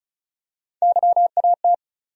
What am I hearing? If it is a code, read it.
Morse code, heard as a keyed tone, about 36 wpm, 700 Hz. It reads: YAT
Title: USA Call Sign Postfixes - 3 Letters 35wpm